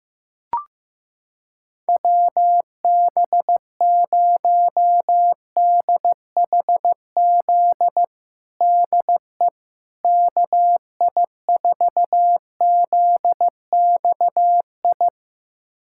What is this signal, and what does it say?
Morse code, heard as a keyed tone, about 15 wpm, 700 Hz. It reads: WB0DHZ DE KI4ZXI
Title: USA Callsign DE Callsign - 6 Characters - 15wpm